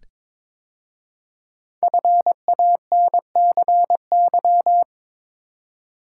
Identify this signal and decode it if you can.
Morse code, heard as a keyed tone, about 22 wpm, 700 Hz. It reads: FANCY
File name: 5 Letter Words - First 3 Letters Given - 22wpm